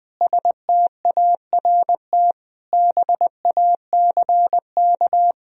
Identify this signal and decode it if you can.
Morse code, heard as a keyed tone, about 20 wpm, 700 Hz. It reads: START BACK
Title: Sentences from Top 1000 Words - Review Part 1 - 20wpm